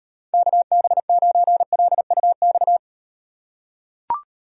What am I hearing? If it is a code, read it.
Morse code, heard as a keyed tone, about 36 wpm, 700 Hz. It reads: KB9LUX